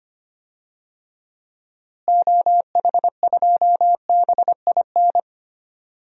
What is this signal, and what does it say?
Morse code, heard as a keyed tone, about 25 wpm, 700 Hz. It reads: OH2BIN
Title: DXCC-100 Call Signs - 6 Characters - Repeated 3x - 25wpm